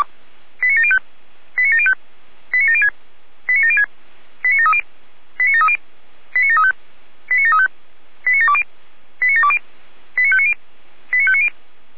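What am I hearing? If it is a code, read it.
Five-tone selective call (ZVEI), heard as quick runs of five tones, tones of about 70 ms. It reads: 89705 89705 89706 89706 89730 89730 89735 89735 89720 89720 89590 89590
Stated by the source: ZVEI five-tone selective call 89750 received by SDR.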